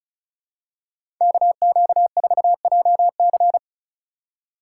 Morse code, heard as a keyed tone, about 36 wpm, 700 Hz. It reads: KQ4JC
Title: USA Call Signs - 5 Characters - 2x Character Spacing - 35wpm